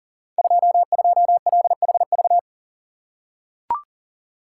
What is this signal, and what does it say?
Morse code, heard as a keyed tone, about 43 wpm, 690 Hz. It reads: 22LHV
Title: Code Groups - 5 Letter:Numbers - 40wpm